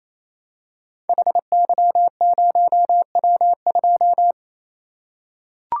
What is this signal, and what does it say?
Morse code, heard as a keyed tone, about 27 wpm, 700 Hz. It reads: HY0W2
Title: Code Groups - 5 Letter:Numbers - 28wpm